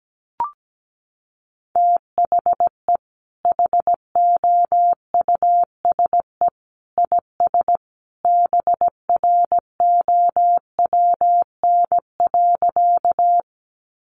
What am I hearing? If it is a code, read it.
Morse code, heard as a keyed tone, about 17 wpm, 700 Hz. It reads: THE HOUSE IS BROWN.
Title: Sentences from Top 1200 Words 17wpm